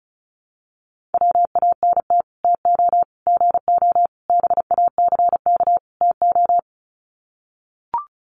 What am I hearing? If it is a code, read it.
Morse code, heard as a keyed tone, about 34 wpm, 700 Hz. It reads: WANT TO GO BACK TO
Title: Sets of 5 Words 35wpm